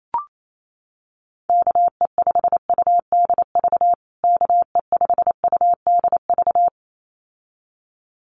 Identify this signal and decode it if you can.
Morse code, heard as a keyed tone, about 28 wpm, 700 Hz. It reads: KE5UDV KE5UDV
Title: USA Call Signs Repeated 2x 28wpm